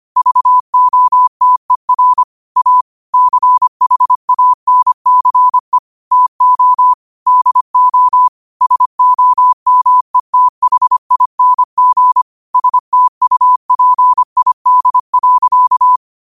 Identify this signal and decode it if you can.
Morse code, heard as a keyed tone, about 25 wpm, 1000 Hz. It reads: VOTER A CHANCE TO DO SOMETHING STUPID.